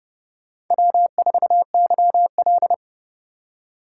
Morse code, heard as a keyed tone, about 26 wpm, 700 Hz. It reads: W4YL